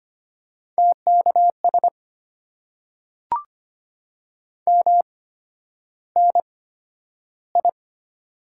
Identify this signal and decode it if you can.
Morse code, heard as a keyed tone, about 25 wpm, spaced as for 11 wpm, 700 Hz. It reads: TKS MNI
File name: Top 90 QSO Elements - 8x Character Spacing - 25wpm